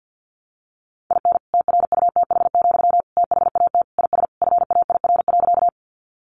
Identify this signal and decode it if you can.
Morse code, heard as a keyed tone, about 50 wpm, 700 Hz. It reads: IN TRUTH, THAT IS FAIR.
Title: Sentences from Top 1300 Words 50wpm